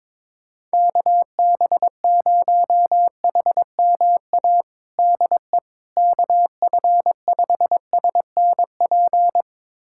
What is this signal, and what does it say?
Morse code, heard as a keyed tone, about 21 wpm, 700 Hz. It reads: KB0HMA DE KF5SNP
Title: USA Callsign DE Callsign - 6 Characters - 22wpm